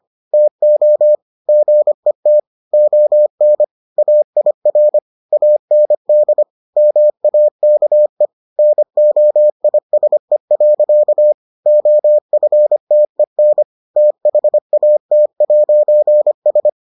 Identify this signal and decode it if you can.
Morse code, heard as a keyed tone, about 25 wpm, 600 Hz. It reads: TO GET ON AIR AND MAKE NOISE. OFTEN THAT'S